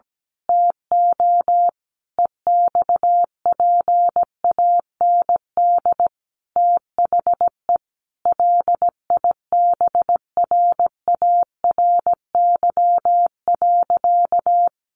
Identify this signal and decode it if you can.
Morse code, heard as a keyed tone, about 17 wpm, 700 Hz. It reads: TO EXPAND THE LIBRARY.
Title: Sentences from Top 1400 Words 17wpm